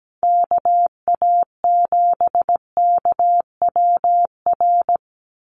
Morse code, heard as a keyed tone, about 17 wpm, 700 Hz. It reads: KA7KWR
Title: USA Call Signs 17wpm